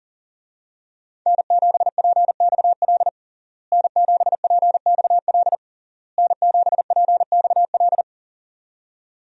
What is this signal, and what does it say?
Morse code, heard as a keyed tone, about 40 wpm, 700 Hz. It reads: N7PXL N7PXL N7PXL
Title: USA Call Signs - 5 Characters - Repeated 3x - 40wpm